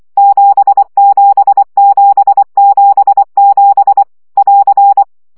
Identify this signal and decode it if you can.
Morse code, heard as a keyed tone, about 24 wpm, 800 Hz. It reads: 77777 <AR>
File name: CW gyakorlas 7 20 baud